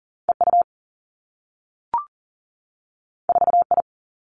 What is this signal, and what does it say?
Morse code, heard as a keyed tone, about 41 wpm, 700 Hz. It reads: EU 4I
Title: Two Challenging Letter-Number Combinations 40wpm